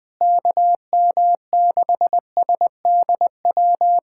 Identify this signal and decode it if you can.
Morse code, heard as a keyed tone, about 21 wpm, 700 Hz. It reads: KM6SDW
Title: USA Callsign DE Callsign - 6 Characters - 20wpm